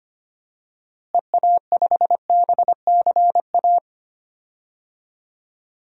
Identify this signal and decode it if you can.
Morse code, heard as a keyed tone, about 25 wpm, 700 Hz. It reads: EA5BCA
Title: DXCC-100 Call Signs - 6 Characters - Repeated 2x - 25wpm